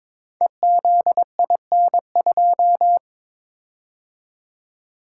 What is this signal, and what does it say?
Morse code, heard as a keyed tone, about 22 wpm, 700 Hz. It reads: EZIN2
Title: Code Groups - 5 Letter:Numbers - 22wpm